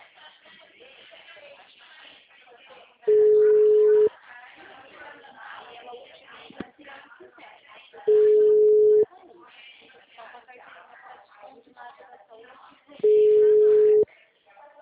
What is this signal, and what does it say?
Ringback tone (Europe, 425 Hz), with speech.